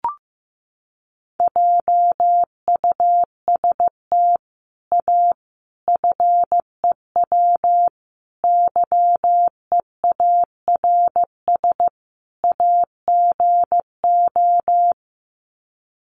Morse code, heard as a keyed tone, about 15 wpm, 700 Hz. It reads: JUST A FEW YEARS AGO
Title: Sets of 5 Words Encore 15wpm